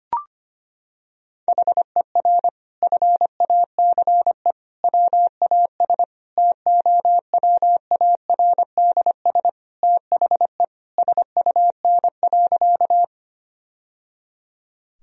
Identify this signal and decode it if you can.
Morse code, heard as a keyed tone, about 25 wpm, 700 Hz. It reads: HER FACE WAS TOWARDS THE SUN.
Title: Sentences from Top 1000 Words - Review Part 2 - 25wpm